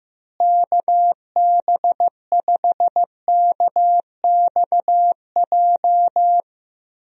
Morse code, heard as a keyed tone, about 15 wpm, 700 Hz. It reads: KB5KXJ
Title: USA Call Signs - 6 Characters - Repeated 3x - 15wpm